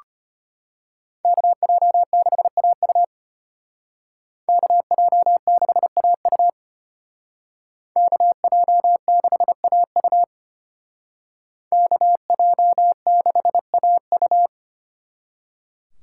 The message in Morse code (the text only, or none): KJ6AU KJ6AU KJ6AU KJ6AU